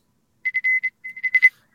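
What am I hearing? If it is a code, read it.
Morse code, heard as a keyed tone, about 26 wpm, spaced as for 16 wpm, 2050 Hz. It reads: FS
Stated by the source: Live amateur radio HF contacts attempted on the air.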